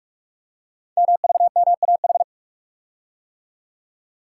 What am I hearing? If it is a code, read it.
Morse code, heard as a keyed tone, about 44 wpm, 700 Hz. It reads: MVMAH
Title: Code Groups - 5 Letters - 45wpm